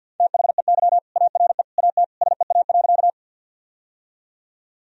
Morse code, heard as a keyed tone, about 51 wpm, 700 Hz. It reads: THEY ARE AT SEA.
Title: Sentences from Top 1000 Words - Review Part 1 - 50wpm